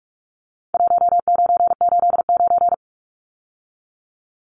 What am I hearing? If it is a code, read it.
Morse code, heard as a keyed tone, about 46 wpm, 700 Hz. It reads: 1989
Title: Numbers - 4-Digit 45wpm